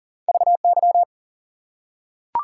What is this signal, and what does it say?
Morse code, heard as a keyed tone, about 38 wpm, 700 Hz. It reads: VY
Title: Top 180 QSO Elements 40wpm